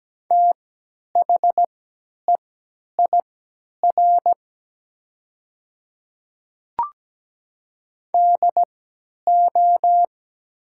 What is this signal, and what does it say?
Morse code, heard as a keyed tone, about 17 wpm, spaced as for 10 wpm, 700 Hz. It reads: THEIR DO
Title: 3x Character Spacing - Top 100 Words - 17wpm